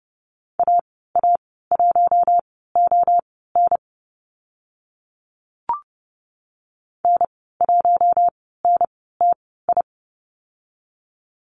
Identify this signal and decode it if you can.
Morse code, heard as a keyed tone, about 30 wpm, spaced as for 17 wpm, 700 Hz. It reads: AA1ON N1NTI
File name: Lesson 14g - Number 2 - Call Signs 30wpm